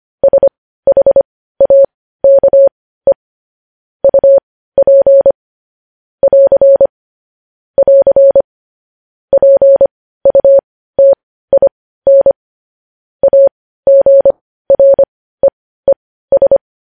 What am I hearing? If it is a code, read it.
Morse code, heard as a keyed tone, about 25 wpm, spaced as for 15 wpm, 550 Hz. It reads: SHAKE UP <AR> <AR> PUTIN AGREES
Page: news headlines in morse code at 15 wpm